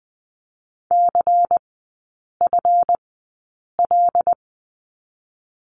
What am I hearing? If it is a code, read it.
Morse code, heard as a keyed tone, about 20 wpm, 700 Hz. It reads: C F L
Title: 184 Lesson 30e - Letter b - Three Characters 20wpm